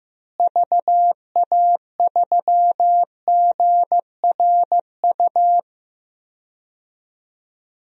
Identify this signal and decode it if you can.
Morse code, heard as a keyed tone, about 15 wpm, 700 Hz. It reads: VA3GRU
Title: DXCC-100 Call Signs - 6 Characters - Repeated 2x - 15wpm